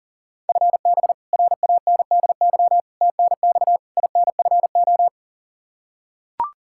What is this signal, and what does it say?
Morse code, heard as a keyed tone, about 40 wpm, 700 Hz. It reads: FB RANDY TNX INFO
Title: K1USN SST QSO Practice 40wpm